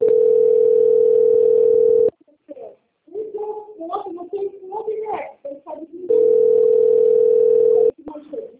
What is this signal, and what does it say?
Ringback tone (North America), 440 + 480 Hz, with speech.
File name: out- - - - - .52757